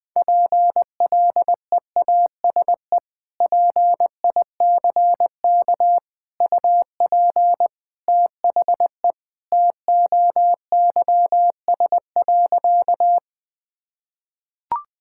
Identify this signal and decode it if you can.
Morse code, heard as a keyed tone, about 21 wpm, 700 Hz. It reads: PLEASE PICK UP THE TOYS.
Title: Sentences with 2-Word Phrases - Spoken Beginning - 20wpm